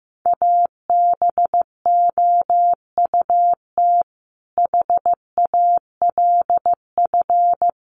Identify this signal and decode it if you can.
Morse code, heard as a keyed tone, about 14 wpm, 700 Hz. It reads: ABOUT HALF
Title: Sentences from Top 600 Words 15wpm